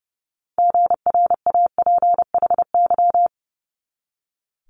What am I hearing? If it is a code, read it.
Morse code, heard as a keyed tone, about 30 wpm, 700 Hz. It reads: GRAPHY
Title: Multi-letter Phonemes 30wpm